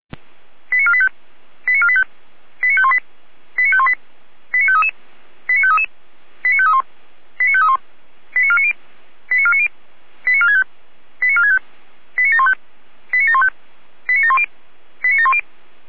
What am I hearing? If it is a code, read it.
Five-tone selective call (ZVEI), heard as quick runs of five tones, tones of about 70 ms. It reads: 89485 89485 89518 89518 89533 89533 89531 89531 89490 89490 89465 89465 89715 89715 89710 89710